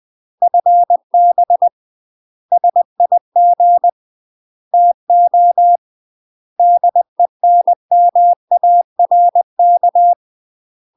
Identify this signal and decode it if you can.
Morse code, heard as a keyed tone, about 20 wpm, 700 Hz. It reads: FB SIG TO DENMARK
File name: UK MASTER QSOs TRAINING/20WPM